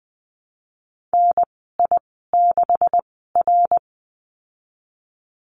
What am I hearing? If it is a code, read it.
Morse code, heard as a keyed tone, about 20 wpm, spaced as for 15 wpm, 700 Hz. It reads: NI6R
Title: 128 Lesson 21g - Number 6 - Call Signs 20wpm